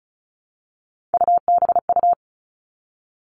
Morse code, heard as a keyed tone, about 36 wpm, 700 Hz. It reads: UBU
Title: Three Challenging Letter-Number Combinations 35wpm